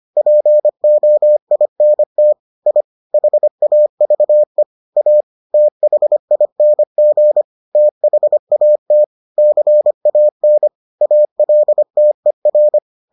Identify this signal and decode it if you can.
Morse code, heard as a keyed tone, about 25 wpm, 600 Hz. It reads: POINT I HAVE A THING THAT CAN ALTER